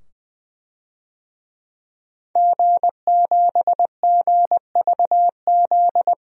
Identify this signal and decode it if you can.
Morse code, heard as a keyed tone, about 20 wpm, 700 Hz. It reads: G7GVZ
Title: DXCC-100 Call Signs - 5 Characters - 20wpm